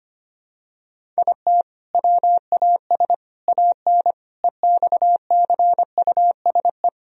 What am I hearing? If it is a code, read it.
Morse code, heard as a keyed tone, about 25 wpm, 700 Hz. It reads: IT WAS AN EXCUSE
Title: Sentences from Top 1000 Words - Review Part 1 - 25wpm